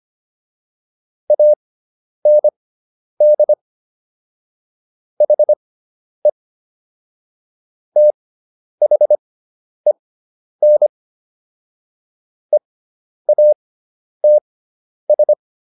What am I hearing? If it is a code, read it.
Morse code, heard as a keyed tone, about 25 wpm, spaced as for 10 wpm, 600 Hz. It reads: AND HE THEN EATS